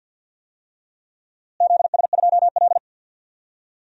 Morse code, heard as a keyed tone, about 47 wpm, 700 Hz. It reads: ZS2L
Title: DXCC-100 Call Signs - 4 Characters - Repeated 3x - 50wpm